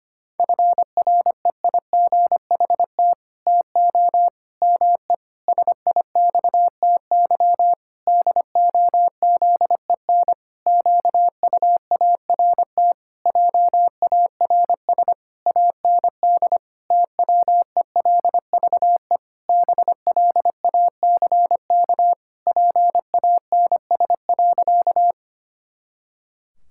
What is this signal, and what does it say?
Morse code, heard as a keyed tone, about 25 wpm, 700 Hz. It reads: FREIGHT TO ME SIXTY DOZEN QUART JARS AND TWELVE BLACK PANS.
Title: Pangram Challenge v2 - 25wpm